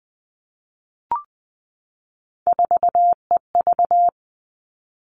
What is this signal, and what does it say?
Morse code, heard as a keyed tone, about 20 wpm, 700 Hz. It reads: 4EV
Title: All Cut Numbers - 3-Digit 20wpm